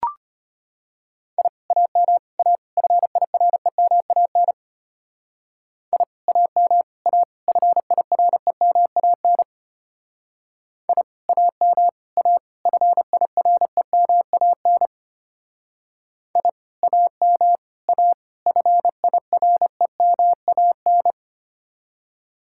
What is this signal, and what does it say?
Morse code, heard as a keyed tone, about 29 wpm, 700 Hz. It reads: I AM A FIREMAN I AM A FIREMAN I AM A FIREMAN I AM A FIREMAN
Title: Speed-Racing Common QSO Phrases - 25wpm